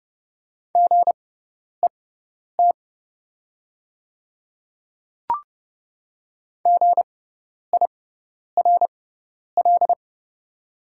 Morse code, heard as a keyed tone, about 30 wpm, spaced as for 10 wpm, 700 Hz. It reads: GET GIRL
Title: 6x Character Spacing - Top 500 Words - 30wpm